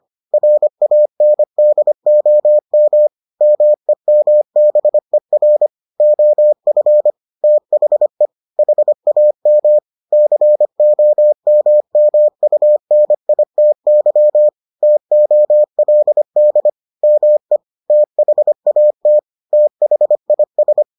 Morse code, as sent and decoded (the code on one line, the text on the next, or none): .-. .- -. -.. --- -- / -- . -- -... .-. / --- ..-. / - .... / .... .- -- / -.-. --- -- -- ..- -. .. - -.-- / - --- .-.. -.. / -- . / - .... .- - / - .... .. ...
RANDOM MEMBER OF THE HAM COMMUNITY TOLD ME THAT THIS